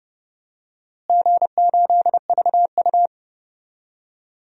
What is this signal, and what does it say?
Morse code, heard as a keyed tone, about 30 wpm, 700 Hz. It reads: G8VU
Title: DXCC-100 Call Signs - 4 Characters - Repeated 3x - 30wpm